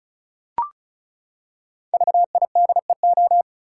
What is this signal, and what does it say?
Morse code, heard as a keyed tone, about 34 wpm, 700 Hz. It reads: VIDEO
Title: Five Letter Words - 35wpm